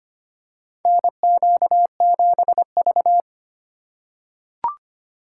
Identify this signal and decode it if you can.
Morse code, heard as a keyed tone, about 25 wpm, 700 Hz. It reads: NQ7V
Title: USA Call Signs - 4 Characters - 3x Character Spacing - 25wpm